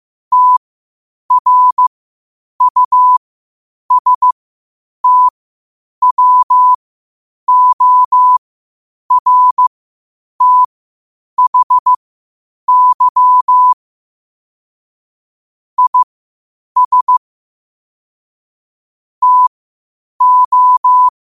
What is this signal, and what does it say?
Morse code, heard as a keyed tone, about 15 wpm, spaced as for 8 wpm, 1000 Hz. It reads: TRUSTWORTHY IS TO
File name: qotdfiles / 8wpm